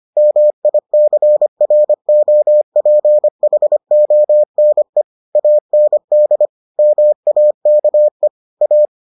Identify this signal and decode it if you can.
Morse code, heard as a keyed tone, about 25 wpm, 600 Hz. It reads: MICROPHONE AND MAKE A